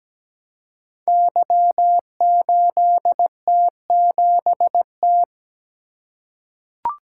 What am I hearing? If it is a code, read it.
Morse code, heard as a keyed tone, about 17 wpm, 700 Hz. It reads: Y8T7T
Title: Code Groups - 5 Letter:Numbers - 17wpm